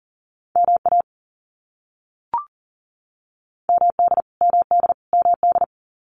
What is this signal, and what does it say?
Morse code, heard as a keyed tone, about 40 wpm, 700 Hz. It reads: MA MD MD MD
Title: US States Repeated 3x 40wpm